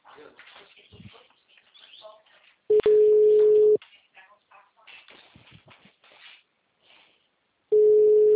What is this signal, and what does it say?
Ringback tone (Europe, 425 Hz).